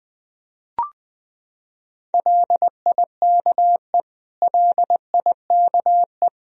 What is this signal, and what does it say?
Morse code, heard as a keyed tone, about 20 wpm, 700 Hz. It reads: LIKE LIKE
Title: Top 100 Words Repeated 20wpm